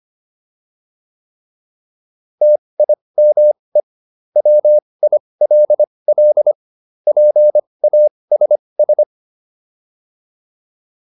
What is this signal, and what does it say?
Morse code, heard as a keyed tone, about 25 wpm, spaced as for 20 wpm, 600 Hz. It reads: TIME WILL PASS
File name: PR301 20